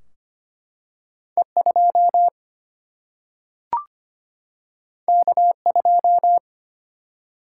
Morse code, heard as a keyed tone, about 25 wpm, 700 Hz. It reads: E 2 K 2
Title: Single-Letter Single-Number 25wpm